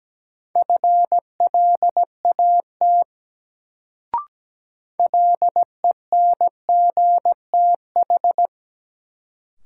Morse code, heard as a keyed tone, about 17 wpm, 700 Hz. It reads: FLAT LENGTH
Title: Sentences from Top 1000 Words - New Word Drill - 17wpm